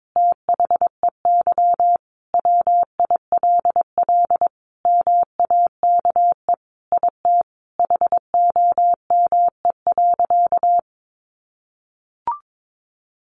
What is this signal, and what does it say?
Morse code, heard as a keyed tone, about 22 wpm, 700 Hz. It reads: THEY WILL MAKE IT HOME.
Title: Sentences from Top 1500 Words 22wpm